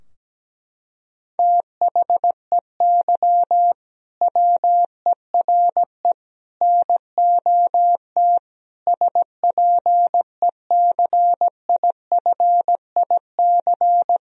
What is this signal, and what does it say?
Morse code, heard as a keyed tone, about 17 wpm, 700 Hz. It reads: THEY WERE NOT SPECIFIC
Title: Sentences from Top 1000 Words - Review Part 3 - 17wpm